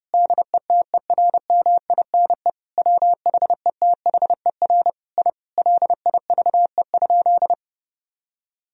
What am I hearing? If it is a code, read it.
Morse code, heard as a keyed tone, about 31 wpm, 700 Hz. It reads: DETERMINE WHETHER I LIVE?
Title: Sentences from Top 1000 Words - Review Part 4 - 30wpm